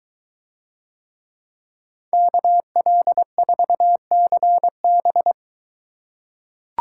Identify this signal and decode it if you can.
Morse code, heard as a keyed tone, about 22 wpm, 700 Hz. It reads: KL4CB